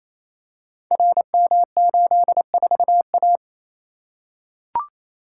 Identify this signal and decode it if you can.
Morse code, heard as a keyed tone, about 29 wpm, 690 Hz. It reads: RM84A
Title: Code Groups - 5 Letter:Numbers - 28wpm